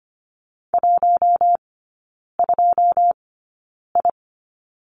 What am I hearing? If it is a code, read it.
Morse code, heard as a keyed tone, about 25 wpm, spaced as for 9 wpm, 690 Hz. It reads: 12I
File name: Lesson 15e - Number 5 - Three Characters 25wpm